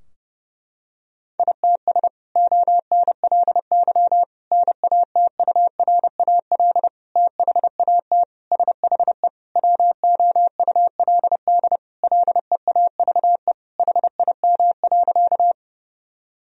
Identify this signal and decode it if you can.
Morse code, heard as a keyed tone, about 30 wpm, 700 Hz. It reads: ITS ONLY NATURAL THAT SHE WOULD LEAVE HIM.
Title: Sentences from Top 1000 Words - Review Part 4 - 30wpm